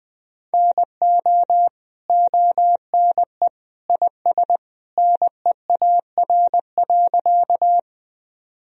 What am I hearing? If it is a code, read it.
Morse code, heard as a keyed tone, about 20 wpm, 700 Hz. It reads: NO ONE IS NEAR.
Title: Sentences from Top 1000 Words - Review Part 1 - 20wpm